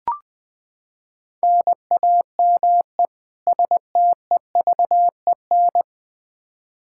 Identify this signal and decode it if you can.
Morse code, heard as a keyed tone, about 20 wpm, 700 Hz. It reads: NAME STEVEN
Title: Practice QSO - Exchange 1 - Beginning - 20wpm